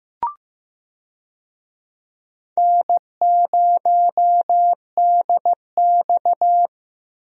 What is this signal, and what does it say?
Morse code, heard as a keyed tone, about 15 wpm, 700 Hz. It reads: N0DX
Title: USA Call Signs - 4 Characters - Repeated 2x - 15wpm